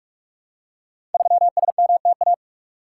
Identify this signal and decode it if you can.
Morse code, heard as a keyed tone, about 46 wpm, 700 Hz. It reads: 3SMTA
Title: Code Groups - 5 Letter:Numbers - 45wpm